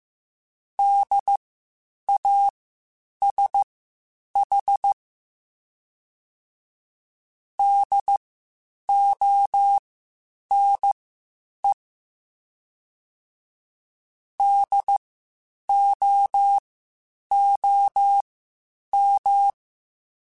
Morse code, heard as a keyed tone, about 15 wpm, spaced as for 8 wpm, 800 Hz. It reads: DASH DONE DOOM
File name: Spaced Alphas / Spaced Alphas mp3s / Ds sp @13